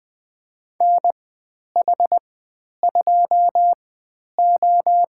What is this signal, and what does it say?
Morse code, heard as a keyed tone, about 20 wpm, spaced as for 10 wpm, 700 Hz. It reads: NH2O
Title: 095 Lesson 16g - Letter c - Call Signs 20wpm